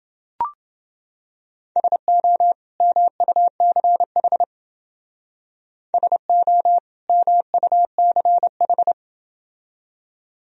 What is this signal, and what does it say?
Morse code, heard as a keyed tone, about 29 wpm, 700 Hz. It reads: SO MUCH SO MUCH